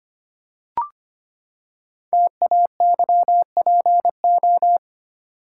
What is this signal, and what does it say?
Morse code, heard as a keyed tone, about 25 wpm, 700 Hz. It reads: TAYPO
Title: Code Groups - 5 Letters - 25wpm